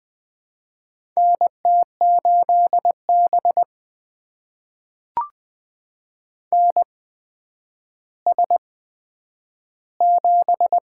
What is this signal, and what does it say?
Morse code, heard as a keyed tone, about 20 wpm, spaced as for 10 wpm, 700 Hz. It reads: NT8B NS7
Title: USA Call Signs - 4 Characters - 8x Character Spacing - 20wpm